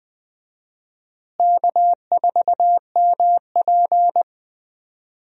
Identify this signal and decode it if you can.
Morse code, heard as a keyed tone, about 20 wpm, 700 Hz. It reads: K4MP
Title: USA Call Signs - 4 Characters - Repeated 3x - 20wpm